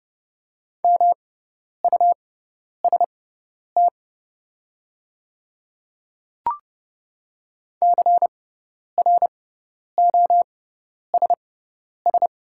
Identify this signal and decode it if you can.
Morse code, heard as a keyed tone, about 30 wpm, spaced as for 10 wpm, 700 Hz. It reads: MUST CROSS
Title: 6x Character Spacing - Top 500 Words - 30wpm